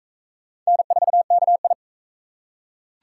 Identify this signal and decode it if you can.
Morse code, heard as a keyed tone, about 38 wpm, 700 Hz. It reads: N4KI